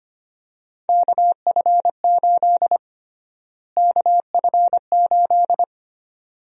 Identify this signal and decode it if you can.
Morse code, heard as a keyed tone, about 25 wpm, 700 Hz. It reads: KF8 KF8
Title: USA Call Sign Prefixes - Repeated 2x - 25wpm